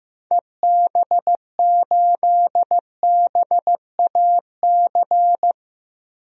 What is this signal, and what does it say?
Morse code, heard as a keyed tone, about 15 wpm, 700 Hz. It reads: EB8BAC
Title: DXCC-100 Call Signs - 6 Characters - Repeated 2x - 15wpm